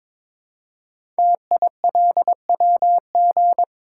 Morse code, heard as a keyed tone, about 22 wpm, 700 Hz. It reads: TILWG
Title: Code Groups - 5 Letters - 22wpm